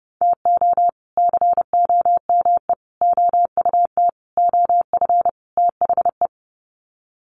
Morse code, heard as a keyed tone, about 30 wpm, 700 Hz. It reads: TO COME OUT OF THE
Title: Sets of 5 Words Encore2 30wpm